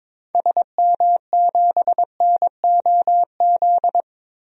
Morse code, heard as a keyed tone, about 22 wpm, 700 Hz. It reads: SM7NOZ
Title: DXCC-100 Call Signs - 6 Characters - Repeated 3x - 22wpm